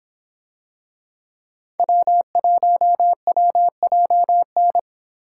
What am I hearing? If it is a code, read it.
Morse code, heard as a keyed tone, about 24 wpm, 700 Hz. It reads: W1WJN